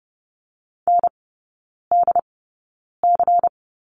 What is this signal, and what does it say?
Morse code, heard as a keyed tone, about 28 wpm, spaced as for 9 wpm, 690 Hz. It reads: NDC
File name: Lesson 16e - Letter c - Three Characters 30wpm